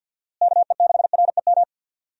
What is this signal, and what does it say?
Morse code, heard as a keyed tone, about 52 wpm, 700 Hz. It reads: KE6REM
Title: USA Call Signs - 6 Characters - Repeated 2x - 50wpm